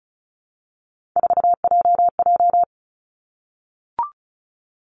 Morse code, heard as a keyed tone, about 35 wpm, 700 Hz. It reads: 4JJ
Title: Three Challenging Letter-Number Combinations 35wpm